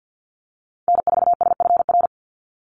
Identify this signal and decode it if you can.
Morse code, heard as a keyed tone, about 50 wpm, 700 Hz. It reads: N4SFR
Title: NA SOTA Activator Callsigns 50wpm